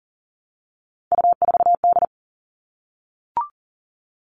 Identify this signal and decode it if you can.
Morse code, heard as a keyed tone, about 40 wpm, 700 Hz. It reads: U4D
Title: All Cut Numbers - 3-Digit 40wpm